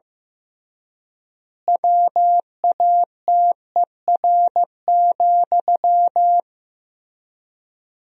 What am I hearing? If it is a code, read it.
Morse code, heard as a keyed tone, about 15 wpm, 700 Hz. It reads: WATER,
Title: 3x Word-Spacing - Sentences from Top 400 Words - 15wpm